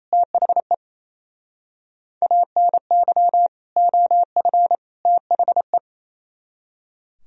Morse code, heard as a keyed tone, about 35 wpm, 700 Hz. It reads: THE ANY OF THE
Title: Speed-Racing - Sets of 3 Words 28wpm